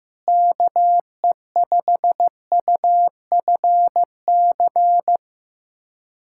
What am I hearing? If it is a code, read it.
Morse code, heard as a keyed tone, about 16 wpm, 700 Hz. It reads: KE5UFC
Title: USA Callsign DE Callsign - 6 Characters - 15wpm